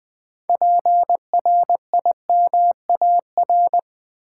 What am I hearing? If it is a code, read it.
Morse code, heard as a keyed tone, about 20 wpm, 700 Hz. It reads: PRIMAR
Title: Build A Word - 3 Letters At A Time - 9-Letter Words - 20wpm